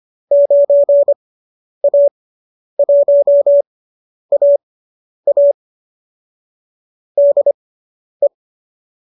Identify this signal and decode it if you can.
Morse code, heard as a keyed tone, about 25 wpm, spaced as for 10 wpm, 590 Hz. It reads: 9A1AA DE
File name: qso219 10